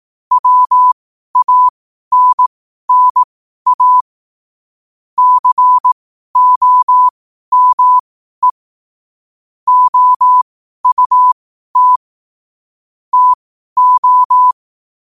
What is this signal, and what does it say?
Morse code, heard as a keyed tone, about 18 wpm, spaced as for 12 wpm, 1000 Hz. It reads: WANNA COME OUT TO
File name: qotdfiles / 12wpm